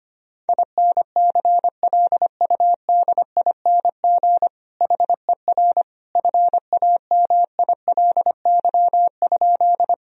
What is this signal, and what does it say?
Morse code, heard as a keyed tone, about 25 wpm, 700 Hz. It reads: INCLUDING HER FAMILY?
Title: Sentences from Top 1000 Words - Review Part 4 - 25wpm